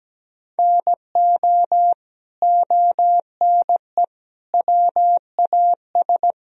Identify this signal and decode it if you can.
Morse code, heard as a keyed tone, about 17 wpm, 700 Hz. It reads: NO ONE WAS
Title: Sentences from Top 1000 Words - Review Part 4 - 17wpm